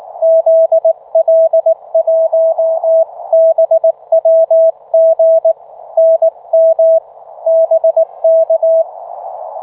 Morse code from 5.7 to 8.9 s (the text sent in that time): NM BK